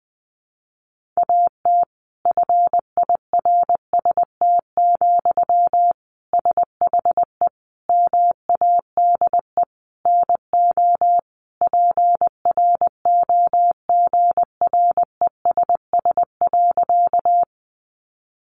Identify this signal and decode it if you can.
Morse code, heard as a keyed tone, about 20 wpm, 700 Hz. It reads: AT FIRST, SHE MADE NO PROGRESS.
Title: Sentences from Top 1400 Words 20wpm